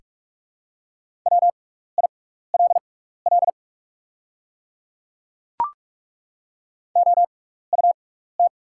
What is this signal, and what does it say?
Morse code, heard as a keyed tone, about 43 wpm, spaced as for 15 wpm, 690 Hz. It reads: WILL OUT
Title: 6x Character Spacing - Top 100 Words - 45wpm